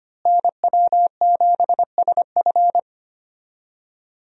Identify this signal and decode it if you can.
Morse code, heard as a keyed tone, about 25 wpm, 700 Hz. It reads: NW7SF
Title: USA Call Signs - 5 Characters - Repeated 3x - 25wpm